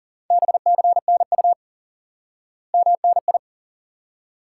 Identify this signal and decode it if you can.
Morse code, heard as a keyed tone, about 37 wpm, 700 Hz. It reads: BCNU MNI
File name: IQE Course 3 - Lesson 89 - YAGI - Two Elements - 40wpm